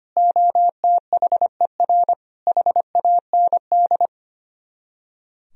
Morse code, heard as a keyed tone, about 33 wpm, 700 Hz. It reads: OTHER HAND